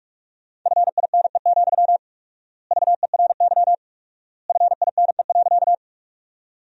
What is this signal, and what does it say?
Morse code, heard as a keyed tone, about 46 wpm, 700 Hz. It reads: FINE, VERY FINE.
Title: 4x Word-Spacing - Sentences from Top 500 Words - 45wpm